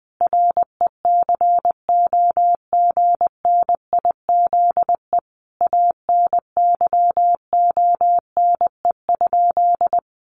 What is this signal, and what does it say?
Morse code, heard as a keyed tone, about 20 wpm, 700 Hz. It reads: RECOGNIZE ANYONE?
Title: Sentences from Top 1400 Words 20wpm